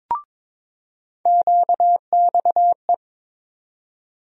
Morse code, heard as a keyed tone, about 22 wpm, 700 Hz. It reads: QXE
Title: USA Call Sign Postfixes - 3 Letters 22wpm